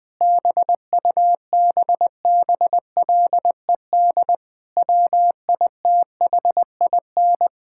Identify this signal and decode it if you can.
Morse code, heard as a keyed tone, about 20 wpm, 700 Hz. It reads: BUBBLED WITHIN
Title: Difficult Sentences with Hints - 20wpm